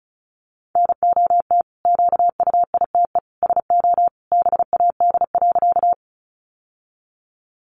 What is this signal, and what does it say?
Morse code, heard as a keyed tone, about 34 wpm, 700 Hz. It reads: NOT QUITE SO BAD.
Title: Sentences from Top 700 Words 35wpm